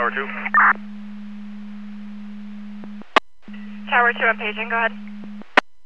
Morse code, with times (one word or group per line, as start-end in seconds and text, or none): none